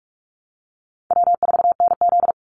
Morse code, heard as a keyed tone, about 44 wpm, 700 Hz. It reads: W4NZ
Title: Easy Call Signs Repeated 3x - 45wpm